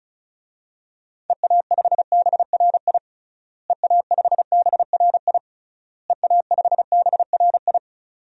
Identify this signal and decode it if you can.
Morse code, heard as a keyed tone, about 36 wpm, 700 Hz. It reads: EA5BRI EA5BRI EA5BRI
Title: DXCC-100 Call Signs - 6 Characters - Repeated 3x - 35wpm